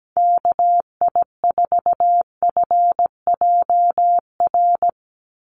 Morse code, heard as a keyed tone, about 17 wpm, 700 Hz. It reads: KI4FJR
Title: USA Call Signs 17wpm